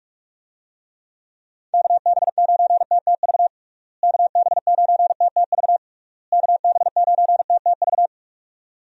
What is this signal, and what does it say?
Morse code, heard as a keyed tone, about 46 wpm, 700 Hz. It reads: KB9TTV KB9TTV KB9TTV
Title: USA Call Signs - 6 Characters - Repeated 3x - 45wpm